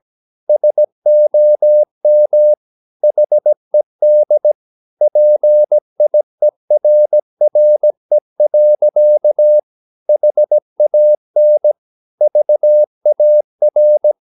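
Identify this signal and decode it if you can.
Morse code, heard as a keyed tone, about 17 wpm, 600 Hz. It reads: SOM HED PIERRE. HAN VAR
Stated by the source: Slow (QRS) Morse practice at 17 wpm, from CW med Gnister.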